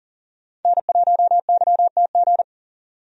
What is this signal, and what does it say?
Morse code, heard as a keyed tone, about 40 wpm, 700 Hz. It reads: N1YTG
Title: USA Call Signs - 5 Characters - 40wpm